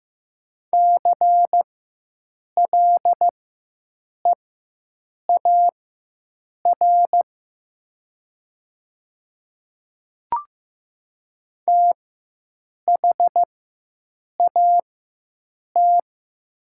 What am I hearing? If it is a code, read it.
Morse code, heard as a keyed tone, about 15 wpm, spaced as for 7 wpm, 700 Hz. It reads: CLEAR THAT
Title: 4x Character Spacing - Top 500 Words - 15wpm